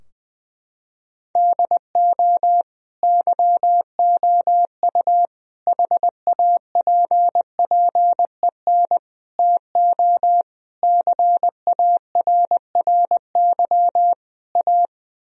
Morse code, heard as a keyed tone, about 20 wpm, 700 Hz. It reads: DO YOU HAPPEN TO CARRY A
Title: Sentences from Top 1000 Words - Review Part 2 - 20wpm